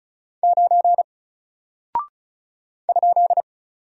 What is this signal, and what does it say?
Morse code, heard as a keyed tone, about 36 wpm, 700 Hz. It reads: 9 ?
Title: Single Character (Letter:Number:Punctuation:Prosign) Alt 35wpm